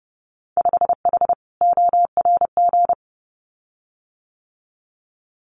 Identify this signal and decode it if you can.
Morse code, heard as a keyed tone, about 30 wpm, 700 Hz. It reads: 5H ORG
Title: Winter Field Day Exchanges Encore 30wpm